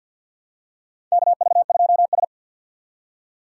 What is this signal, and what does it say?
Morse code, heard as a keyed tone, about 47 wpm, 700 Hz. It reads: KV2S
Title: USA Call Signs - 4 Characters - 3x Character Spacing - 50wpm